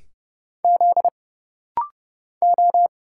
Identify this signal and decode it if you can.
Morse code, heard as a keyed tone, about 31 wpm, 700 Hz. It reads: Z O
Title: Single Letter-Number - Rapid-Fire 30wpm